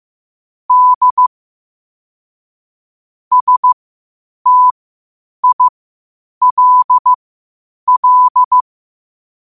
Morse code, heard as a keyed tone, about 15 wpm, spaced as for 8 wpm, 1000 Hz. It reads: D STILL
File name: qotdfiles / 8wpm